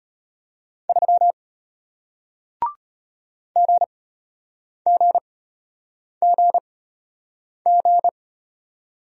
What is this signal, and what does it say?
Morse code, heard as a keyed tone, about 33 wpm, spaced as for 8 wpm, 700 Hz. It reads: 3 GGGG